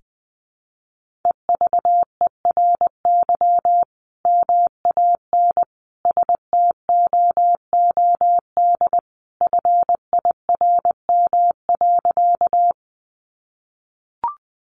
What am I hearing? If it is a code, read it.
Morse code, heard as a keyed tone, about 20 wpm, 700 Hz. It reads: EVERY MAN STOOD FIRM.
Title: Sentences from Top 1400 Words 20wpm